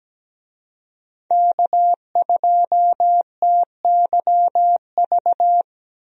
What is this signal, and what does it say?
Morse code, heard as a keyed tone, about 17 wpm, 700 Hz. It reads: K2TYV
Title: USA Call Signs - 5 Characters - Repeated 2x - 17wpm